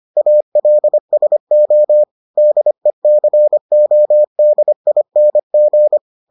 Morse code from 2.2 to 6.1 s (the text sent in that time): DECODING